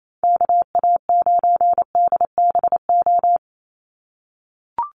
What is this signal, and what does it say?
Morse code, heard as a keyed tone, about 28 wpm, 700 Hz. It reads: KA9DBO
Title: USA Call Signs 28wpm